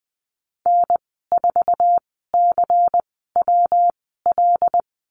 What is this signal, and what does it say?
Morse code, heard as a keyed tone, about 20 wpm, spaced as for 15 wpm, 700 Hz. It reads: N4CWL
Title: 154 Lesson 25g - Letter p - Call Signs 20wpm